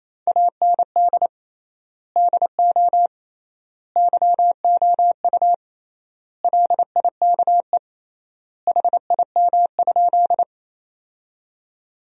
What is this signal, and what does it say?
Morse code, heard as a keyed tone, about 28 wpm, 700 Hz. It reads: AND DO YOU LIKE HIM?
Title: 3x Word-Spacing - Sentences from Top 100 Words - 28wpm